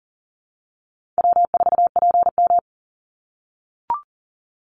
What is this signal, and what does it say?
Morse code, heard as a keyed tone, about 38 wpm, 700 Hz. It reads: W4PM
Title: Call Signs - 40wpm